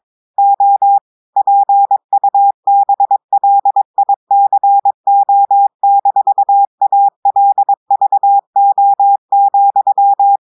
Morse code, heard as a keyed tone, about 22 wpm, 800 Hz. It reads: O PUBLICO-ALVO,
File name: audios / portugais / portugais texte en clair 2 vitesse 22 mots minutes